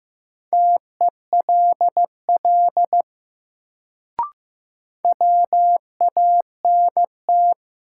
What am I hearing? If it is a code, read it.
Morse code, heard as a keyed tone, about 15 wpm, 700 Hz. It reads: TELL WANT
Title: Sets of 2 Words - New Word Drill - 15wpm